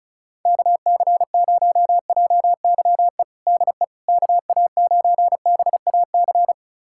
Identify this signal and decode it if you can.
Morse code, heard as a keyed tone, about 36 wpm, 700 Hz. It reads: KC0JYE DE KA9BAC
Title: USA Callsign DE Callsign - 6 Characters - 35wpm